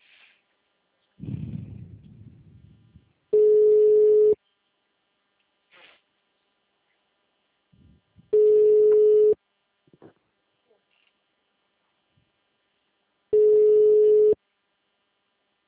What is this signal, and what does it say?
Ringback tone (Europe, 425 Hz).
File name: out- -2049- - - .6567